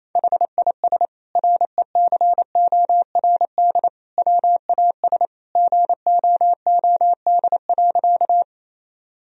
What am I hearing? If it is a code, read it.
Morse code, heard as a keyed tone, about 27 wpm, 700 Hz. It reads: HIS RECORD WAS GOOD.
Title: Sentences from Top 1000 Words - Review Part 3 - 28wpm